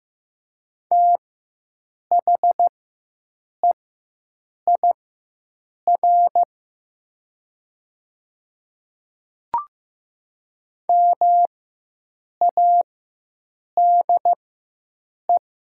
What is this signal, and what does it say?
Morse code, heard as a keyed tone, about 15 wpm, spaced as for 7 wpm, 700 Hz. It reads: THEIR MADE
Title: 4x Character Spacing - Top 200 Words - 15wpm